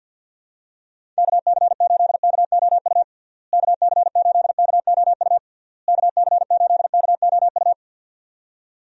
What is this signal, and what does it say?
Morse code, heard as a keyed tone, about 49 wpm, 700 Hz. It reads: KC8KOU KC8KOU KC8KOU
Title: USA Call Signs - 6 Characters - Repeated 3x - 50wpm